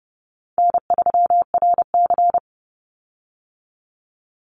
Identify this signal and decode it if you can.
Morse code, heard as a keyed tone, about 30 wpm, 700 Hz. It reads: N3RC
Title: Easy Call Signs Repeated 2x - 30wpm